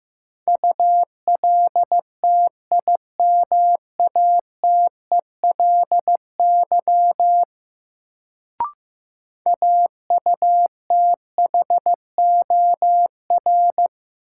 Morse code, heard as a keyed tone, about 15 wpm, 700 Hz. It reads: ULTIMATELY AUTHOR
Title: Sentences from Top 1200 Words - New Word Drill - 15wpm